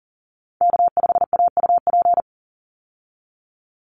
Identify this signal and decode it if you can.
Morse code, heard as a keyed tone, about 40 wpm, 700 Hz. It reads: K5AUP
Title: Call Signs Repeated 3x - 40wpm